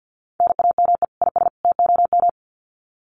Morse code, heard as a keyed tone, about 50 wpm, 700 Hz. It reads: NAME IS TOM
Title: Common QSO Phrases 50wpm